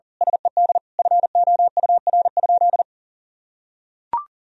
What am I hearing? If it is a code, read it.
Morse code, heard as a keyed tone, about 40 wpm, 700 Hz. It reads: SED FOUR?
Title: Sentences from Top 300 Words and Abbreviations 40wpm